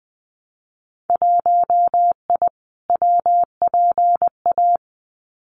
Morse code, heard as a keyed tone, about 20 wpm, 700 Hz. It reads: 1I WPA
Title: Winter Field Day Exchanges Encore 20wpm